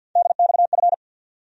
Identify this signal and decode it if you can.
Morse code, heard as a keyed tone, about 47 wpm, 700 Hz. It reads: DXF
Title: USA Call Sign Postfixes - 3 Letters 50wpm